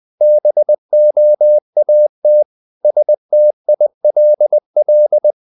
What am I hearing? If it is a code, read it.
Morse code, heard as a keyed tone, about 20 wpm, 600 Hz. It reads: BOAT STILL
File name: the old man and the sea / the old man and the sea part 1 20wpm